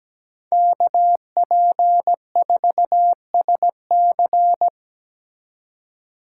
Morse code, heard as a keyed tone, about 20 wpm, 700 Hz. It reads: KP4SC
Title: Speed Racing - USA 2x2 Call Signs 17wpm